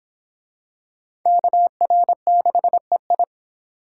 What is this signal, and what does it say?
Morse code, heard as a keyed tone, about 21 wpm, 700 Hz. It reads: KR6EI
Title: Speed Racing - USA Call Signs 17wpm